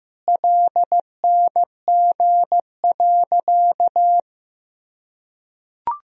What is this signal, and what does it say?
Morse code, heard as a keyed tone, about 15 wpm, 700 Hz. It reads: LNG.
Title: Sentences from Top 100 Words and Abbreviations 15wpm